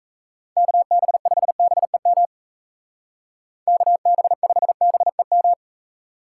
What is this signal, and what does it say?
Morse code, heard as a keyed tone, about 33 wpm, 700 Hz. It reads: KB5BEM KB5BEM
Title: Speed Racing - USA 2x3 Call Signs 28wpm